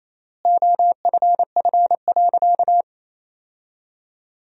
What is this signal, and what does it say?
Morse code, heard as a keyed tone, about 28 wpm, 700 Hz. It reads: OFF.
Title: 3x Word-Spacing - Sentences from Top 500 Words - 28wpm